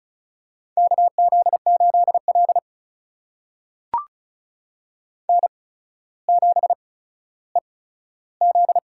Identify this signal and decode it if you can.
Morse code, heard as a keyed tone, about 36 wpm, spaced as for 15 wpm, 700 Hz. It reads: KZ8L N7EZ
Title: USA Call Signs - 4 Characters - 8x Character Spacing - 35wpm